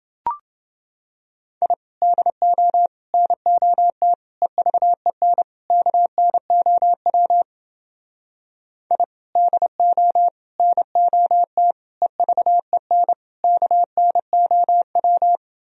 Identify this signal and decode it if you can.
Morse code, heard as a keyed tone, about 29 wpm, 700 Hz. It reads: I DO NOT EVEN KNOW I DO NOT EVEN KNOW